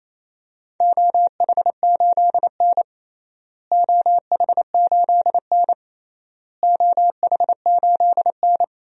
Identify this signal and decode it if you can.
Morse code, heard as a keyed tone, about 28 wpm, 700 Hz. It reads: OH8N OH8N OH8N
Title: DXCC-100 Call Signs - 4 Characters - Repeated 3x - 28wpm